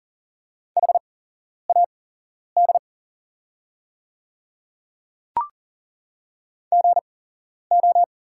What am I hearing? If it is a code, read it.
Morse code, heard as a keyed tone, about 40 wpm, spaced as for 11 wpm, 700 Hz. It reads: HAD GO
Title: 8x Character Spacing - Top 200 Words - 40wpm